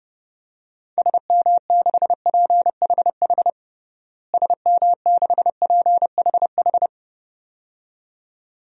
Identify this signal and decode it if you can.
Morse code, heard as a keyed tone, about 32 wpm, 700 Hz. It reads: SM6PHH SM6PHH
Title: DXCC-100 Call Signs - 6 Characters - Repeated 2x - 30wpm